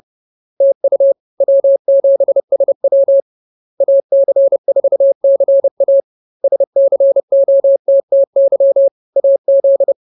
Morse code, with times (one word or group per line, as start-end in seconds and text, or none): 0.6-1.1 TU
1.4-3.2 W7SW
3.8-6.0 AC4CA
6.4-8.9 SCOTTY
9.2-9.9 AZ